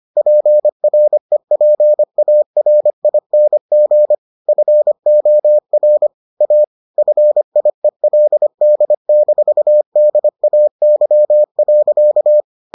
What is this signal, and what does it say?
Morse code, heard as a keyed tone, about 25 wpm, 600 Hz. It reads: PREPARING FOR A FIELD-DAY.